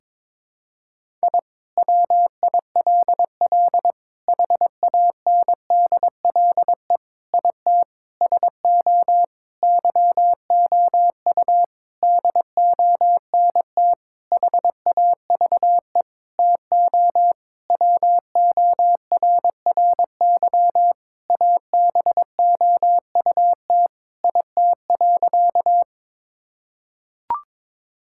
Morse code, heard as a keyed tone, about 22 wpm, 700 Hz. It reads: I WILL HANDLE IT SO YOU DONT HAVE TO WORRY ABOUT IT.
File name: Sentences with 7-Word Phrases - 22wpm